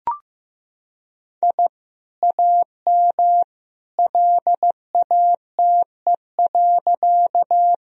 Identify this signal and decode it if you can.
Morse code, heard as a keyed tone, about 15 wpm, 700 Hz. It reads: I AM LATE.
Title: Sentences from Top 1000 Words - Review Part 2 - 15wpm